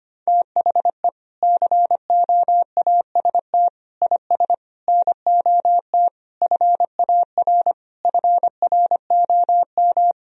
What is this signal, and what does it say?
Morse code, heard as a keyed tone, about 25 wpm, 700 Hz. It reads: THE COAST IS NOT FAR FROM
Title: Sentences from Top 1000 Words - Review Part 4 - 25wpm